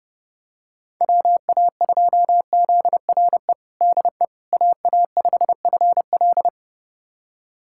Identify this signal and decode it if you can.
Morse code, heard as a keyed tone, about 30 wpm, 700 Hz. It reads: WA2ZRE DE AA5FL
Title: USA Callsign DE Callsign 30wpm